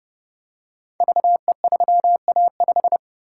Morse code, heard as a keyed tone, about 28 wpm, 700 Hz. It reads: VE3A5
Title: Code Groups - 5 Letter:Numbers - 30wpm